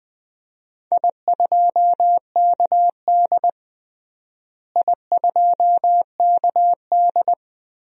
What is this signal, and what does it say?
Morse code, heard as a keyed tone, about 20 wpm, 700 Hz. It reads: I2KD I2KD
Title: DXCC-100 Call Signs - 4 Characters - Repeated 2x - 20wpm